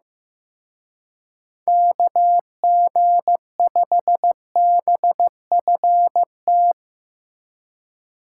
Morse code, heard as a keyed tone, about 16 wpm, 700 Hz. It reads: KG5BFT
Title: Speed Racing - USA 2x3 Call Signs 15wpm